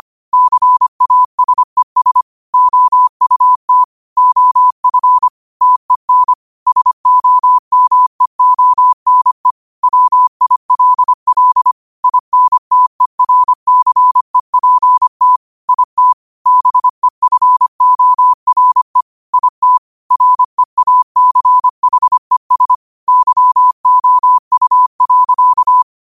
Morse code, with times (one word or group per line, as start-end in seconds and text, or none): 0.3-2.2 CASES
2.5-3.8 OUT
4.2-5.3 OF
5.6-6.3 TEN
6.7-9.5 SOMEONE
9.8-11.7 WILL
12.0-15.4 INTERCEPT
15.7-16.1 IT
16.5-19.0 BEFORE
19.3-19.8 IT
20.1-22.8 REACHES
23.1-25.8 YOU.